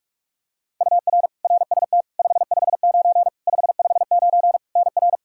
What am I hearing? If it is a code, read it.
Morse code, heard as a keyed tone, about 46 wpm, 700 Hz. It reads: UR RST 559 559 NR
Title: Practice QSO - Exchange 1 - Beginning - 45wpm